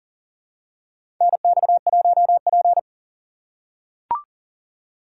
Morse code, heard as a keyed tone, about 40 wpm, 700 Hz. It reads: NX1P
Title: USA Call Signs - 4 Characters - 6x Character Spacing - 40wpm